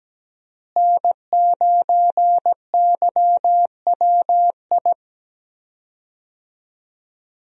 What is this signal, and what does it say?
Morse code, heard as a keyed tone, about 17 wpm, 700 Hz. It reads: N9YWI
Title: USA Call Signs - 5 Characters - Repeated 3x - 17wpm